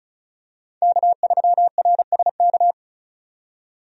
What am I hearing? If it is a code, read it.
Morse code, heard as a keyed tone, about 34 wpm, 700 Hz. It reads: K3RSK
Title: USA Call Signs - 5 Characters - 35wpm